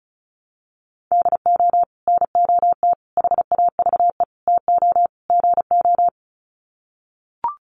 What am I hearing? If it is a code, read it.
Morse code, heard as a keyed tone, about 34 wpm, 700 Hz. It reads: DO NOT HAVE TO GO
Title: Sets of 5 Words 35wpm